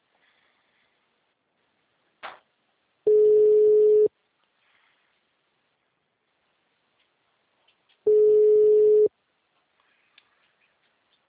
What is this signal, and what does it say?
Ringback tone (Europe, 425 Hz).